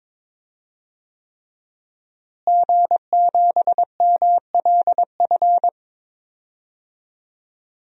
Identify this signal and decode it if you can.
Morse code, heard as a keyed tone, about 21 wpm, 700 Hz. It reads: G7MLF
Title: DXCC-100 Call Signs - 5 Characters - Repeated 3x - 22wpm